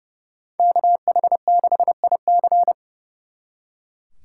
Morse code, heard as a keyed tone, about 37 wpm, 700 Hz. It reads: KH6IC
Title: Speed Racing - USA 2x2 Call Signs 30wpm